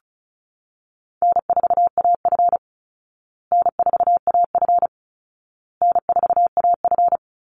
Morse code, heard as a keyed tone, about 34 wpm, 700 Hz. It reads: N4AF N4AF N4AF
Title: Call Signs Repeated 2x - 35wpm